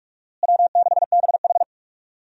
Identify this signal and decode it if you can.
Morse code, heard as a keyed tone, about 46 wpm, 700 Hz. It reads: W6BH
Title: USA Call Signs - 4 Characters - Repeated 2x - 45wpm